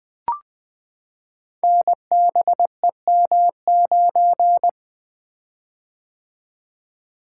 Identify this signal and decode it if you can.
Morse code, heard as a keyed tone, about 20 wpm, 700 Hz. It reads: NBEM9
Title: Code Groups - 5 Letter:Numbers - 20wpm